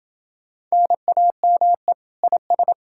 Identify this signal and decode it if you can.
Morse code, heard as a keyed tone, about 30 wpm, 700 Hz. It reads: NAME IS